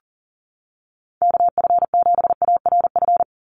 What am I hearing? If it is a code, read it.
Morse code, heard as a keyed tone, about 38 wpm, 700 Hz. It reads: KF7ARF
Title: USA Call Signs 40wpm